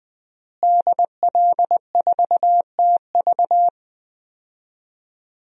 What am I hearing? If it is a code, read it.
Morse code, heard as a keyed tone, about 20 wpm, 700 Hz. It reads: DL4TV
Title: DXCC-100 Call Signs - 5 Characters - Repeated 3x - 20wpm